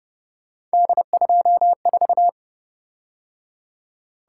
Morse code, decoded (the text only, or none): D24